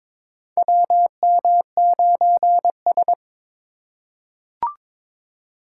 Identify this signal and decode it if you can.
Morse code, heard as a keyed tone, about 23 wpm, 700 Hz. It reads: WM9S
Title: USA Call Signs - 4 Characters - 4x Character Spacing - 22wpm